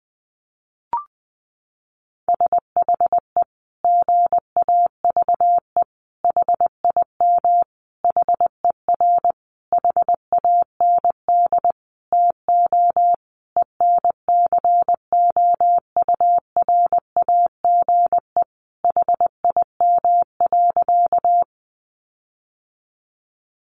Morse code, heard as a keyed tone, about 20 wpm, 700 Hz. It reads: SHE GAVE HIM HER HAND TO ENCOURAGE HIM.
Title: Sentences from Top 1400 Words 20wpm